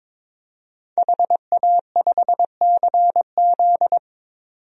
Morse code, heard as a keyed tone, about 22 wpm, 700 Hz. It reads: HA5CZ
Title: DXCC-100 Call Signs - 5 Characters - 22wpm